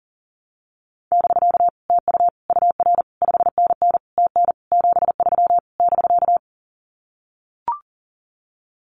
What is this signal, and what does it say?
Morse code, heard as a keyed tone, about 40 wpm, 700 Hz. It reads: <BK> TU UR 5NN TN 73 <BK>
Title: Common QSO Phrases Encore 40wpm